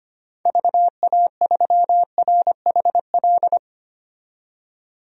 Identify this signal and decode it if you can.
Morse code, heard as a keyed tone, about 25 wpm, 700 Hz. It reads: VA3RHL
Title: DXCC-100 Call Signs - 6 Characters - 25wpm